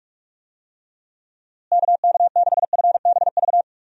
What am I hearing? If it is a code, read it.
Morse code, heard as a keyed tone, about 46 wpm, 700 Hz. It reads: KK6FBV
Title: USA Call Signs - 6 Characters - Repeated 3x - 45wpm